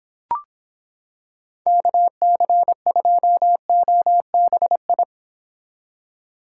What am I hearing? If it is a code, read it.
Morse code, heard as a keyed tone, about 26 wpm, 700 Hz. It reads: KC2OBI